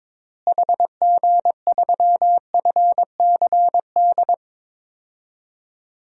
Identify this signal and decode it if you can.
Morse code, heard as a keyed tone, about 22 wpm, 700 Hz. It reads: HG3FCD
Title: DXCC-100 Call Signs - 6 Characters - Repeated 2x - 22wpm